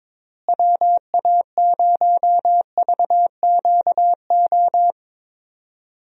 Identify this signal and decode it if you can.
Morse code, heard as a keyed tone, about 22 wpm, 700 Hz. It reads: WA0VQO
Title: USA Call Signs - 6 Characters - Repeated 3x - 22wpm